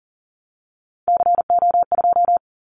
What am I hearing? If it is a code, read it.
Morse code, heard as a keyed tone, about 39 wpm, 700 Hz. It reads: CO2
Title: Callsign Prefixes for Top 100 DXCC - 40wpm